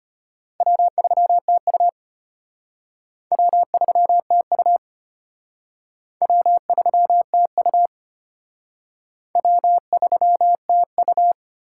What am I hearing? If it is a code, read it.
Morse code, heard as a keyed tone, about 33 wpm, 700 Hz. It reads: W3TU W3TU W3TU W3TU